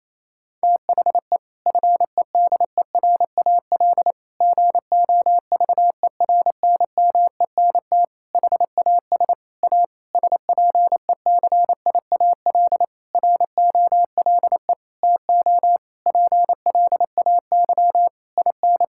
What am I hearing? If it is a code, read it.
Morse code, heard as a keyed tone, about 27 wpm, 700 Hz. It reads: THE FEDERAL GOVERNMENT HAS A SPECIAL ROLE TO PLAY IN
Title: Sentences from Top 1000 Words - Review Part 6 - 28wpm